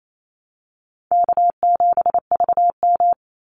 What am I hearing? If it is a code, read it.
Morse code, heard as a keyed tone, about 28 wpm, 700 Hz. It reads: K7VM
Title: Call Signs - 28wpm